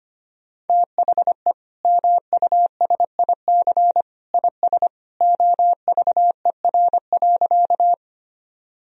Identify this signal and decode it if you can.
Morse code, heard as a keyed tone, about 25 wpm, 700 Hz. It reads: THE MUSIC IS OVER.
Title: Sentences from Top 1000 Words - Review Part 1 - 25wpm